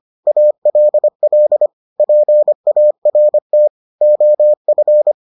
Morse code, heard as a keyed tone, about 25 wpm, 600 Hz. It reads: ALL PART OF